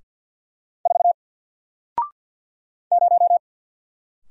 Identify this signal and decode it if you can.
Morse code, heard as a keyed tone, about 50 wpm, 700 Hz. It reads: 4 0
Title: Single Character (Letter:Number:Punctuation:Prosign) Alt 50wpm